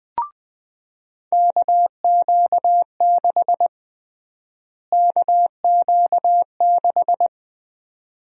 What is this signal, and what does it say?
Morse code, heard as a keyed tone, about 21 wpm, 700 Hz. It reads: KQ6 KQ6
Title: USA Call Sign Prefixes - Repeated 2x - 20wpm